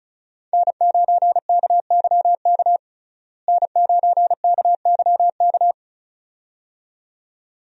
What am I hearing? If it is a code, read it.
Morse code, heard as a keyed tone, about 34 wpm, 700 Hz. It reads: N9KYK N9KYK
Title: USA Call Signs - 5 Characters - Repeated 2x - 35wpm